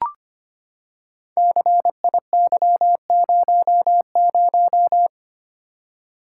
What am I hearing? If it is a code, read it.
Morse code, heard as a keyed tone, about 25 wpm, 700 Hz. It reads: CIY00
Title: Code Groups - 5 Letter:Numbers - 25wpm